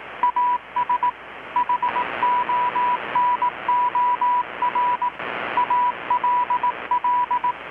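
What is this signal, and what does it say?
Morse code, heard as a keyed tone, about 18 wpm, 980 Hz, with radio noise. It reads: AS HONOR ALL
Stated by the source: Morse from KPH's Night of Nights XX maritime radiotelegraphy event, received in Chiapas, Mexico about 3,800 km away on a Kenwood R-600 with a grounded 10 m long-wire antenna.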